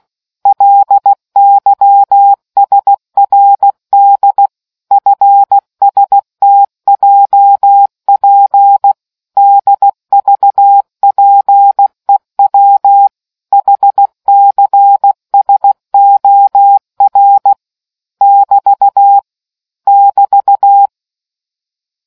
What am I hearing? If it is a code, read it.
Morse code, heard as a keyed tone, about 16 wpm, 800 Hz. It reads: LYSRD FSTJP DVPEW HCSOR <BT> <BT>